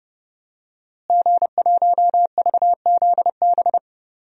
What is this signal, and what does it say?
Morse code, heard as a keyed tone, about 32 wpm, 700 Hz. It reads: G1VZB
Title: DXCC-100 Call Signs - 5 Characters - Repeated 3x - 30wpm